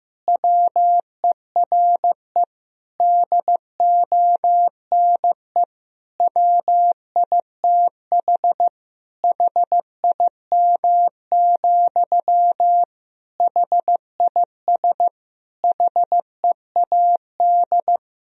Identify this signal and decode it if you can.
Morse code, heard as a keyed tone, about 15 wpm, 700 Hz. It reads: WERE DONE WITH HIM, HIS HEAD
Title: Sentences from Top 1000 Words - Review Part 2 - 15wpm